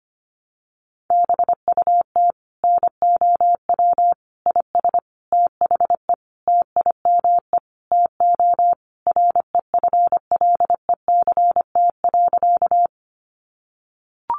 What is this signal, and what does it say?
Morse code, heard as a keyed tone, about 25 wpm, 700 Hz. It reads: BUT NOW IS THE TIME TO REFLECT.
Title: Sentences from Top 1500 Words 25wpm